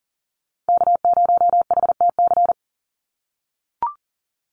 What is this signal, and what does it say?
Morse code, heard as a keyed tone, about 40 wpm, 700 Hz. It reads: K0HTC
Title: USA Call Signs 40wpm